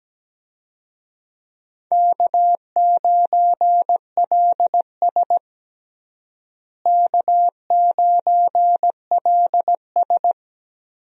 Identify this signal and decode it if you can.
Morse code, heard as a keyed tone, about 17 wpm, 700 Hz. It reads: K9LS K9LS
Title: USA Call Signs - 4 Characters - Repeated 2x - 17wpm